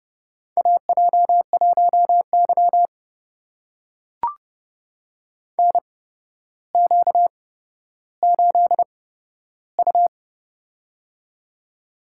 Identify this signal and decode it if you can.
Morse code, heard as a keyed tone, about 30 wpm, spaced as for 13 wpm, 700 Hz. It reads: AJ1Y NQ8U
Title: USA Call Signs - 4 Characters - 8x Character Spacing - 30wpm